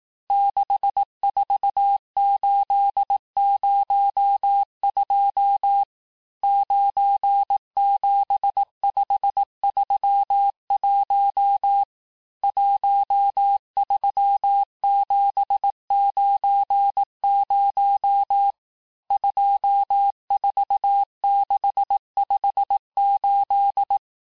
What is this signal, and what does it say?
Morse code, heard as a keyed tone, about 18 wpm, 800 Hz. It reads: 64802 97531 13790 24658
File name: cours-du-lundi files / cours-1588-texte-chiffre-vitesse-17MPM